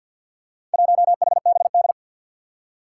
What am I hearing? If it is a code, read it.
Morse code, heard as a keyed tone, about 52 wpm, 700 Hz. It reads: 1HBD
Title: Four Challenging Letter-Number Combinations 50wpm